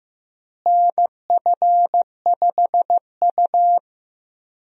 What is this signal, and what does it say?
Morse code, heard as a keyed tone, about 16 wpm, 700 Hz. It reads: NF5U
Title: USA Call Signs - 4 Characters - Repeated 2x - 15wpm